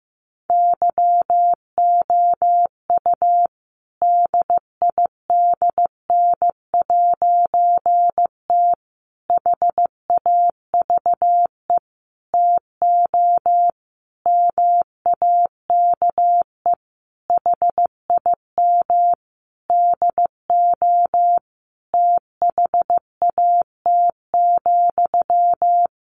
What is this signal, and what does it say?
Morse code, heard as a keyed tone, about 15 wpm, 700 Hz. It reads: YOU DIDN'T HAVE TO MAKE HIM DO THAT,
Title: A Sound Of Thunder 15wpm